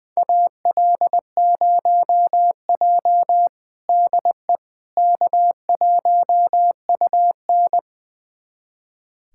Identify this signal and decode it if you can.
Morse code, heard as a keyed tone, about 20 wpm, 700 Hz. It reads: AL0J DE K1UN
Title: USA Callsign DE Callsign - 4 Characters - 20wpm